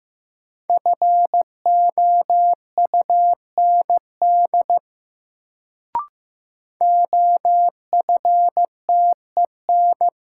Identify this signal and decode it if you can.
Morse code, heard as a keyed tone, about 15 wpm, 700 Hz. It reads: FOUND OFTEN
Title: Sentences from Top 300 Words - New Word Drill - 15wpm